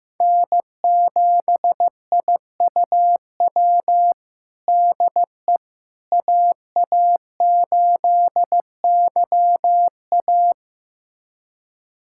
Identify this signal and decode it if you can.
Morse code, heard as a keyed tone, about 16 wpm, 700 Hz. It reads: N7IUW DE AA8YA
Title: USA Callsign DE Callsign - 5 Characters - 15wpm